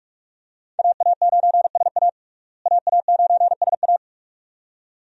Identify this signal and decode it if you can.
Morse code, heard as a keyed tone, about 41 wpm, 700 Hz. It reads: AA9SA AA9SA
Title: USA Call Signs - 5 Characters - Repeated 2x - 45wpm